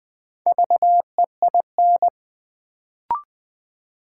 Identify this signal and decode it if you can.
Morse code, heard as a keyed tone, about 20 wpm, 700 Hz. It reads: VEIN
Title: Build A Word - 2 Letters At A Time - 4-Letter Words - 20wpm